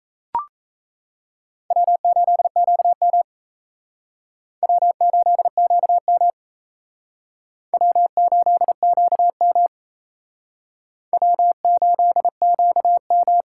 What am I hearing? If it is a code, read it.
Morse code, heard as a keyed tone, about 35 wpm, 700 Hz. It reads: W8QM W8QM W8QM W8QM
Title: Speed Racing - USA 1x2 Call Signs 28wpm